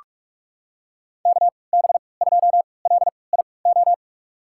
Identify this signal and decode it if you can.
Morse code, heard as a keyed tone, about 45 wpm, spaced as for 25 wpm, 700 Hz. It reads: KB2LIO
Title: USA Call Signs - 6 Characters - 3x Character Spacing - 45wpm